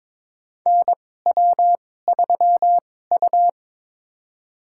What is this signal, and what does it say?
Morse code, heard as a keyed tone, about 23 wpm, spaced as for 16 wpm, 700 Hz. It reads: NW3U
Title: USA Call Signs - 4 Characters - 2x Character Spacing - 22wpm